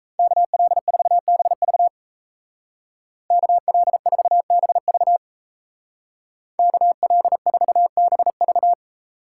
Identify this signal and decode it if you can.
Morse code, heard as a keyed tone, about 35 wpm, 700 Hz. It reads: KL4BV KL4BV KL4BV
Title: Speed Racing - USA 2x2 Call Signs 28wpm